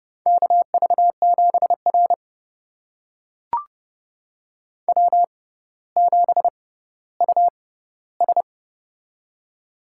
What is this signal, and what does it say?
Morse code, heard as a keyed tone, about 30 wpm, spaced as for 15 wpm, 700 Hz. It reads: KV7R W7US
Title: USA Call Signs - 4 Characters - 6x Character Spacing - 30wpm